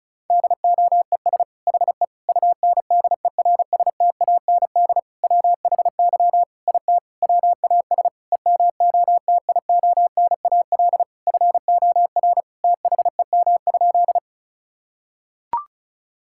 Morse code, heard as a keyed tone, about 36 wpm, 700 Hz. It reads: DOES HE UNDERSTAND WHY IT WAS EMOTIONAL FOR THEM?
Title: Sentences from Top 1000 Words - Review Part 5 - 35wpm